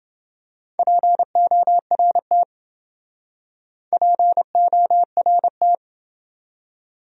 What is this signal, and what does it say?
Morse code, heard as a keyed tone, about 28 wpm, 700 Hz. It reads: PORT PORT